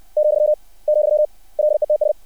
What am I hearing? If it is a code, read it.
Morse code, heard as a keyed tone, about 60 wpm, 600 Hz. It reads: , , LETS
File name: archived / CRSnet